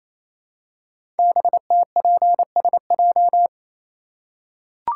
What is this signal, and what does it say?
Morse code, heard as a keyed tone, about 27 wpm, 700 Hz. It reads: BTPSJ
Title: Code Groups - 5 Letters - 28wpm